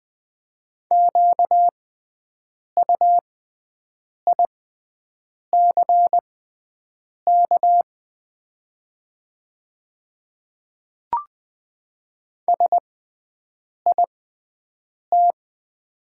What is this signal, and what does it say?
Morse code, heard as a keyed tone, about 20 wpm, spaced as for 7 wpm, 700 Hz. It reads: QUICK SIT
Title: 6x Character Spacing - Top 500 Words - 20wpm